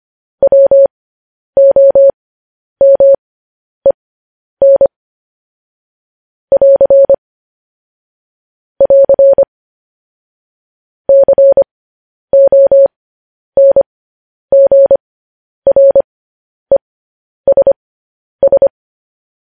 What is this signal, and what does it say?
Morse code, heard as a keyed tone, about 25 wpm, spaced as for 10 wpm, 550 Hz. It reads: WOMEN <AR> <AR> CONGRESS
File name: podcast / News headlines-1x-25x10